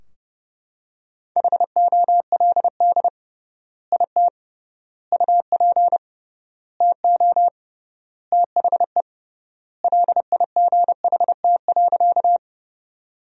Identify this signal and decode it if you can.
Morse code, heard as a keyed tone, about 30 wpm, 700 Hz. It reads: HOLD IT UP TO THE LIGHT.
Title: 3x Word-Spacing - Sentences from Top 400 Words - 30wpm